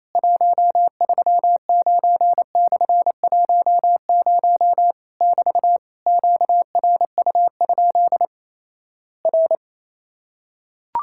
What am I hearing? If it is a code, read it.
Morse code, heard as a keyed tone, about 28 wpm, 700 Hz. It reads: 139/10 <BT> QRU? R
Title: Worked All Europe DX Contest - QTC Exchange - 28wpm